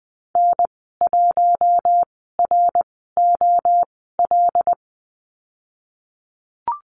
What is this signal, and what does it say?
Morse code, heard as a keyed tone, about 20 wpm, spaced as for 15 wpm, 700 Hz. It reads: N1ROL
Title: 088 Lesson 15g - Number 5 - Call Signs 20wpm